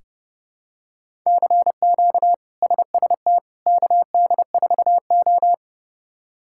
Morse code, heard as a keyed tone, about 30 wpm, 700 Hz. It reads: CQ SST KD4O
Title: K1USN SST QSO Practice - 30wpm